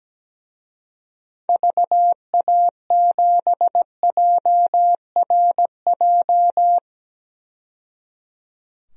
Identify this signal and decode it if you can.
Morse code, heard as a keyed tone, about 17 wpm, 700 Hz. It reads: VA7JRJ
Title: DXCC-100 Call Signs - 6 Characters - 17wpm